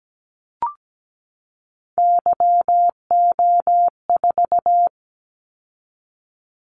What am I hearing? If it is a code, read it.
Morse code, heard as a keyed tone, about 17 wpm, 700 Hz. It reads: YO4
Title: Callsign Prefixes for Top 25 DXCC - 17wpm